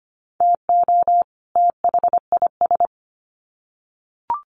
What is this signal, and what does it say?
Morse code, heard as a keyed tone, about 25 wpm, 700 Hz. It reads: TO THIS
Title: Sets of 2 Words 25wpm